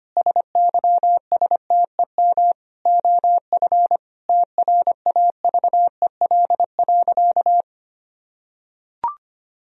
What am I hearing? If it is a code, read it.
Morse code, heard as a keyed tone, about 25 wpm, 700 Hz. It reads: SYSTEM OF TRAVEL.
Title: Sentences from Top 1000 Words - Review Part 4 - 25wpm